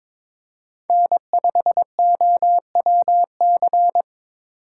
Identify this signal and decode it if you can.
Morse code, heard as a keyed tone, about 25 wpm, 700 Hz. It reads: N5OWC
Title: Speed Racing - USA 1x3 Call Signs 22wpm